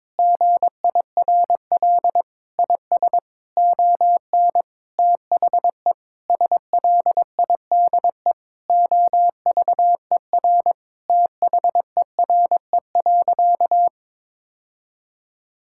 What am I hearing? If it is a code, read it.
Morse code, heard as a keyed tone, about 21 wpm, 700 Hz. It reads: GIRL IS ON THE SLIDE OVER THERE.
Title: Sentences from Top 1600 Words 22wpm